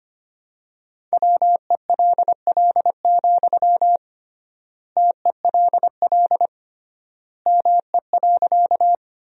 Morse code, heard as a keyed tone, about 25 wpm, 700 Hz. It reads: WELL, TELL ME.
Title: 3x Word-Spacing - Sentences from Top 200 Words - 25wpm